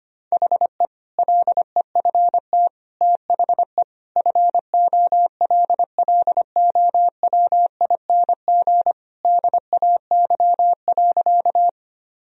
Morse code, heard as a keyed tone, about 25 wpm, 700 Hz. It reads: HE LEFT THE FOLLOWING DAY.
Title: Sentences from Top 1000 Words - Review Part 4 - 25wpm